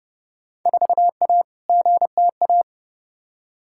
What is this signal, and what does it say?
Morse code, heard as a keyed tone, about 31 wpm, 700 Hz. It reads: 4A GTA
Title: Field Day Exchanges Encore - 30wpm